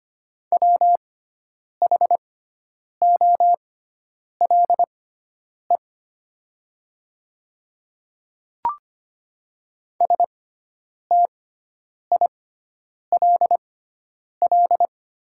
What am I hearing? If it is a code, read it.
Morse code, heard as a keyed tone, about 25 wpm, spaced as for 9 wpm, 700 Hz. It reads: WHOLE STILL
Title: 6x Character Spacing - Top 500 Words - 25wpm